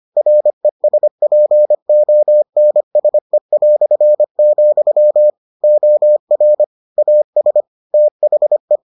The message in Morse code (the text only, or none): RESPONSE", OR AS THE